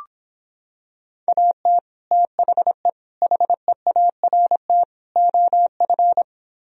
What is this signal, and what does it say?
Morse code, heard as a keyed tone, about 26 wpm, 700 Hz. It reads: AT THE HEART OF